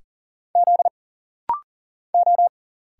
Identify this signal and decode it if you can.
Morse code, heard as a keyed tone, about 40 wpm, 700 Hz. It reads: Z O
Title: Single Letter-Number Alt - Rapid-Fire 40wpm